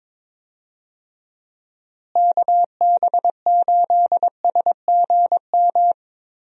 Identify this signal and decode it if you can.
Morse code, heard as a keyed tone, about 21 wpm, 700 Hz. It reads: KB8SGM
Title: USA Call Signs - 6 Characters - Repeated 3x - 22wpm